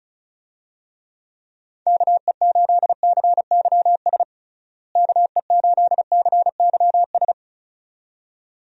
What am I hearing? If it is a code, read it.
Morse code, heard as a keyed tone, about 36 wpm, 700 Hz. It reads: KE8CYS KE8CYS
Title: USA Call Signs - 6 Characters - Repeated 2x - 35wpm